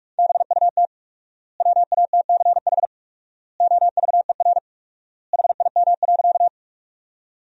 Morse code, heard as a keyed tone, about 42 wpm, 700 Hz. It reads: BUT WATCH OVER HIM.
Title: 4x Word-Spacing - Sentences from Top 300 Words - 45wpm